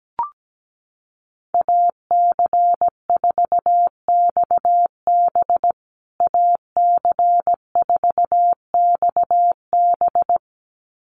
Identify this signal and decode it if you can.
Morse code, heard as a keyed tone, about 17 wpm, 700 Hz. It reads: AC4XB AC4XB
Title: USA Call Signs Repeated 2x 17wpm